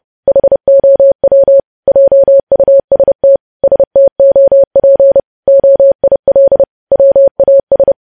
Morse code, heard as a keyed tone, about 30 wpm, 550 Hz. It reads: HOW JUST STOP OIL WAS
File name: podcast / News headlines-1x-30x30